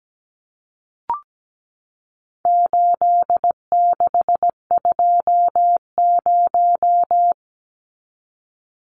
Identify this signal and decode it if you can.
Morse code, heard as a keyed tone, about 17 wpm, 700 Hz. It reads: 8620
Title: Numbers - 4-Digit 17wpm